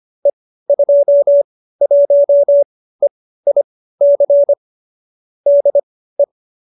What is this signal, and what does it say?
Morse code, heard as a keyed tone, about 25 wpm, spaced as for 15 wpm, 570 Hz. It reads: E21EIC DE